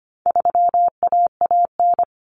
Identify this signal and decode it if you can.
Morse code, heard as a keyed tone, about 25 wpm, 700 Hz. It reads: 3AAN
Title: Common Cut Numbers - 4-Digit 25wpm